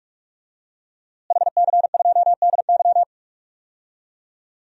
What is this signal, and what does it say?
Morse code, heard as a keyed tone, about 46 wpm, 700 Hz. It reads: HC2DY
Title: DXCC-100 Call Signs - 5 Characters - Repeated 3x - 45wpm